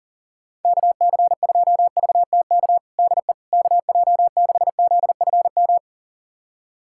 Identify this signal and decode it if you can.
Morse code, heard as a keyed tone, about 40 wpm, 700 Hz. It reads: KC2VTK DE KJ6ZFM
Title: USA Callsign DE Callsign 40wpm